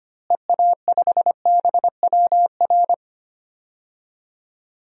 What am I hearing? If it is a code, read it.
Morse code, heard as a keyed tone, about 25 wpm, 700 Hz. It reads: EA5BWR
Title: DXCC-100 Call Signs - 6 Characters - Repeated 3x - 25wpm